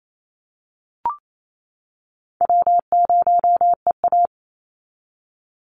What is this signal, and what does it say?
Morse code, heard as a keyed tone, about 28 wpm, 700 Hz. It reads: W0EA
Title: Easy Call Signs - 28wpm